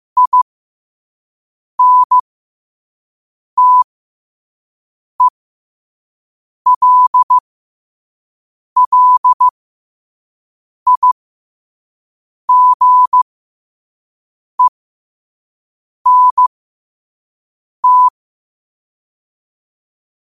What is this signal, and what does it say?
Morse code, heard as a keyed tone, about 15 wpm, spaced as for 5 wpm, 1000 Hz. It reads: INTELLIGENT
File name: qotdfiles / 5wpm